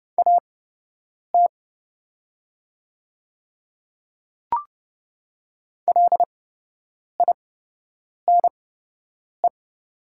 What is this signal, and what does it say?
Morse code, heard as a keyed tone, about 31 wpm, spaced as for 8 wpm, 700 Hz. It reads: AT LINE
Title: 8x Character Spacing - Top 200 Words - 30wpm